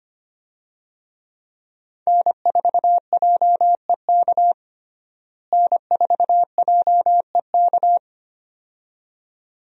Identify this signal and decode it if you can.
Morse code, heard as a keyed tone, about 25 wpm, 700 Hz. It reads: N4JEK N4JEK
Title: USA Call Signs - 5 Characters - Repeated 2x - 25wpm